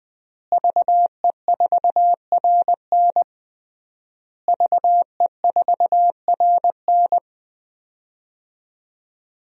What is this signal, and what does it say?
Morse code, heard as a keyed tone, about 21 wpm, 700 Hz. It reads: VE4RN VE4RN
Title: DXCC-100 Call Signs - 5 Characters - Repeated 3x - 20wpm